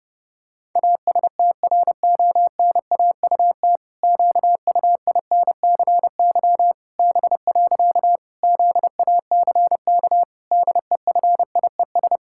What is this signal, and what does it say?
Morse code, heard as a keyed tone, about 30 wpm, 700 Hz. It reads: ASTRONAUT QUINCY B. ZACK DEFIES
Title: Pangram Challenge v2 - 30wpm